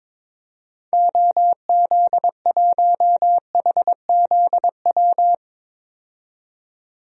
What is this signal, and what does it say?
Morse code, heard as a keyed tone, about 21 wpm, 700 Hz. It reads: OZ1HZW
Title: DXCC-100 Call Signs - 6 Characters - 22wpm